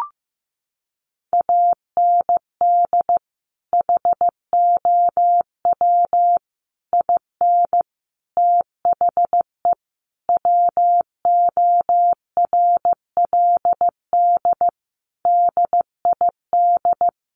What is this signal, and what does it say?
Morse code, heard as a keyed tone, about 15 wpm, 700 Hz. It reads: AND HOW IN THE WORLD DID
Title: Sentences from Top 200 Words 15wpm